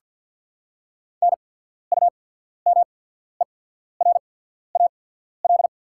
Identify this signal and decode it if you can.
Morse code, heard as a keyed tone, about 49 wpm, spaced as for 14 wpm, 700 Hz. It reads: NUMERAL
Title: 8x Character Spacing - Top 500 Words - 50wpm